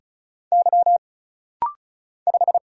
Morse code, heard as a keyed tone, about 35 wpm, 700 Hz. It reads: Y 5
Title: Single Letter-Number - Rapid-Fire 35wpm